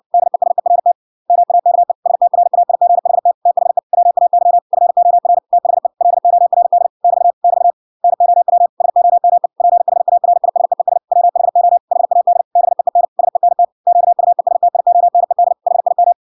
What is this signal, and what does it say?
Morse code, heard as a keyed tone, about 57 wpm, 700 Hz. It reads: DESERT MAKE STRAIGHT THE WAY FOR THE LORD <BT> <BT> NOW SOME PHARISEES WHO HAD BEEN SENT QUESTIONED HIM